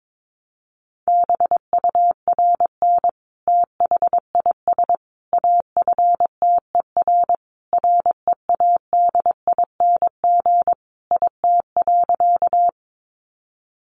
Morse code, heard as a keyed tone, about 22 wpm, 700 Hz. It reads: BURN THIS AFTER READING IT.
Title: Sentences from Top 1300 Words 22wpm